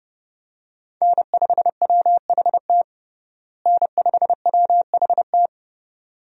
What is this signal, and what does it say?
Morse code, heard as a keyed tone, about 31 wpm, 700 Hz. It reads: N5WHT N5WHT
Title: USA Call Signs - 5 Characters - Repeated 2x - 30wpm